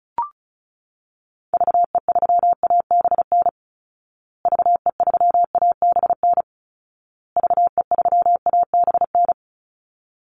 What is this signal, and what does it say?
Morse code, heard as a keyed tone, about 34 wpm, 700 Hz. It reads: VE3ABN VE3ABN VE3ABN
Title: Call Signs Repeated 2x - 35wpm